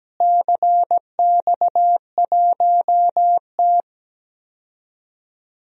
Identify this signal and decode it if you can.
Morse code, heard as a keyed tone, about 17 wpm, 700 Hz. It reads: CX1T
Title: DXCC-100 Call Signs - 4 Characters - Repeated 2x - 17wpm